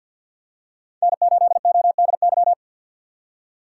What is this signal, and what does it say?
Morse code, heard as a keyed tone, about 48 wpm, 700 Hz. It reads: N8ODY
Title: USA Call Signs - 5 Characters - 6x Character Spacing - 50wpm